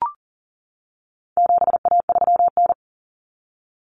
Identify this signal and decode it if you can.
Morse code, heard as a keyed tone, about 40 wpm, 700 Hz. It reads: 7A3N
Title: Common Cut Numbers - 4-Digit 40wpm